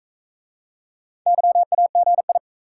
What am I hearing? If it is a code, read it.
Morse code, heard as a keyed tone, about 40 wpm, 700 Hz. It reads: YAGI